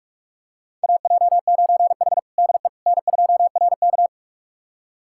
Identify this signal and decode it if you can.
Morse code, heard as a keyed tone, about 46 wpm, 700 Hz. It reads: AJ9H DE N2RK
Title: USA Callsign DE Callsign - 4 Characters - 45wpm